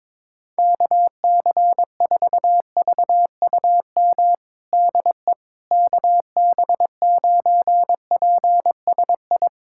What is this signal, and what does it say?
Morse code, heard as a keyed tone, about 22 wpm, 700 Hz. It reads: KC4VUM DE KB9PSI
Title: USA Callsign DE Callsign - 6 Characters - 22wpm